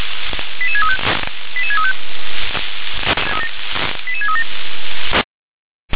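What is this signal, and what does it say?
Five-tone selective call (ZVEI), heard as quick runs of five tones, tones of about 80 ms. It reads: 88635 88635 88637
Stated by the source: Received on SDR, ZVEI five-tone selective call 88633.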